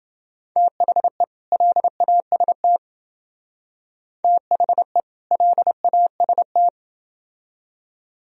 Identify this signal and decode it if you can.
Morse code, heard as a keyed tone, about 29 wpm, 700 Hz. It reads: THE LAST THE LAST